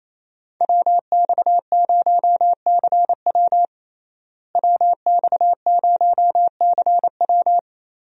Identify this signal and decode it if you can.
Morse code, heard as a keyed tone, about 28 wpm, 700 Hz. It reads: WX0CW WX0CW
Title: USA Call Signs - 5 Characters - Repeated 3x - 28wpm